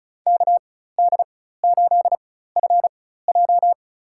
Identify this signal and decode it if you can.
Morse code, heard as a keyed tone, about 36 wpm, spaced as for 16 wpm, 700 Hz. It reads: KD8FJ
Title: USA Call Signs - 4x Character Spacing - 35wpm